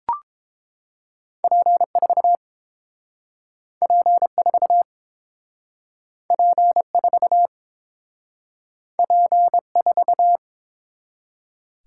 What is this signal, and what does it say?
Morse code, heard as a keyed tone, about 30 wpm, 700 Hz. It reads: P4 P4 P4 P4